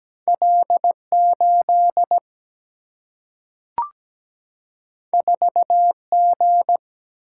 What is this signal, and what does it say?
Morse code, heard as a keyed tone, about 17 wpm, 700 Hz. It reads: L 8 4 G
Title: Single-Letter Single-Number 17wpm